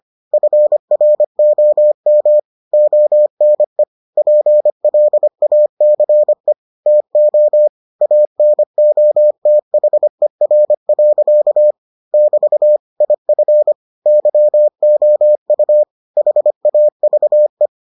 Morse code, heard as a keyed tone, about 25 wpm, 600 Hz. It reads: FROM ONE PLACE TO ANOTHER. <BT> IF YOU HAVE